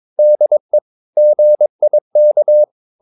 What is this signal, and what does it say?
Morse code, heard as a keyed tone, about 22 wpm, 600 Hz. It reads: DE GIK
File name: Ferie QRS 22wpm